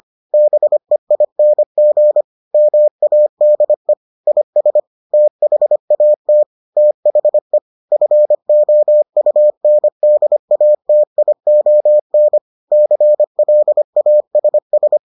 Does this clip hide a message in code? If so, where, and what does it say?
Yes: Morse code from 0.3 to 15.0 s, heard as a keyed tone, about 25 wpm, 600 Hz. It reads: BEING MADE IS THAT THE FOUNDATION CLASS